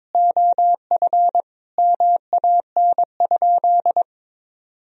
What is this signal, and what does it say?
Morse code, heard as a keyed tone, about 22 wpm, 700 Hz. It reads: OF MAN?
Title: Sentences from Top 1000 Words - Review Part 2 - 22wpm